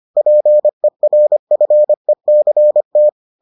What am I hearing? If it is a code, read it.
Morse code, heard as a keyed tone, about 25 wpm, 600 Hz. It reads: PERFECT